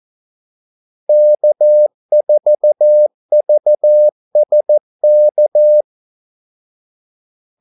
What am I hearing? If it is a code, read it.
Morse code, heard as a keyed tone, about 14 wpm, 600 Hz. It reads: K4VSK
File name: Callsigns-14wpm